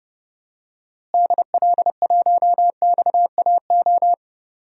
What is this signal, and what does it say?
Morse code, heard as a keyed tone, about 30 wpm, 700 Hz. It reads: DL1XAO
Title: DXCC-100 Call Signs - 6 Characters - Repeated 2x - 30wpm